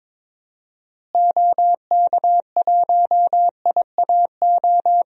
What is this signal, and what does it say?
Morse code, heard as a keyed tone, about 22 wpm, 700 Hz. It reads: OK1IAO
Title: DXCC-100 Call Signs - 6 Characters - Repeated 3x - 22wpm